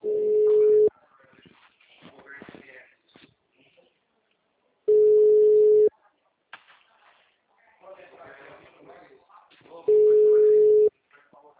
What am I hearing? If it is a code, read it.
Ringback tone (Europe, 425 Hz), with speech.
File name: out- -2014- - - .26396